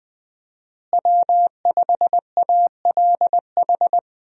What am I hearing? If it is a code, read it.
Morse code, heard as a keyed tone, about 19 wpm, 700 Hz. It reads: W5ALH
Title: USA Call Signs - 5 Characters - Repeated 3x - 20wpm